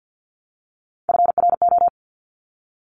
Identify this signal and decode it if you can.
Morse code, heard as a keyed tone, about 50 wpm, 700 Hz. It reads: FRO
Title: Top Three Letter Combinations in Words 50wpm